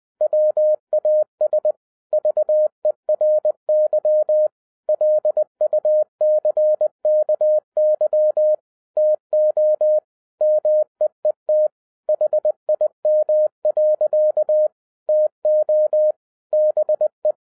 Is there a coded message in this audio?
Yes: Morse code from 0.2 to 17.3 s, heard as a keyed tone, about 20 wpm, 610 Hz. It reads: WAS VERY LUCKY TO MEET HIM. TO BE